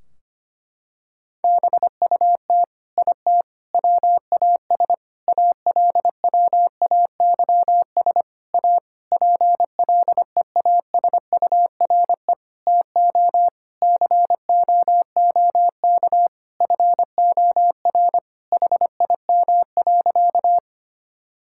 Morse code, heard as a keyed tone, about 25 wpm, 700 Hz. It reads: BUT IT WAS ALWAYS A PLEASURE TO COOK FOR HIM.
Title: Sentences from Top 1600 Words 25wpm